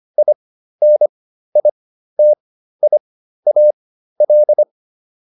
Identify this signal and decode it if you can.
Morse code, heard as a keyed tone, about 25 wpm, spaced as for 13 wpm, 600 Hz. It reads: INITIAL
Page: intermediate-practice-files